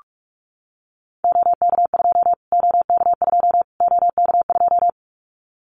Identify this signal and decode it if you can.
Morse code, heard as a keyed tone, about 44 wpm, 700 Hz. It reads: OK2 OK2 OK2
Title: Callsign Prefixes for Top 25 DXCC - Repeated 3x - 45wpm